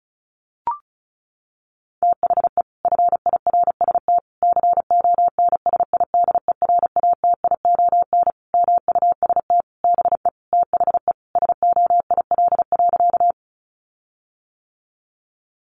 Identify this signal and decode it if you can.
Morse code, heard as a keyed tone, about 34 wpm, 700 Hz. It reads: THE FIRST CONSIDERATION MUST BE THE SOIL.
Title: Sentences from Top 1500 Words 35wpm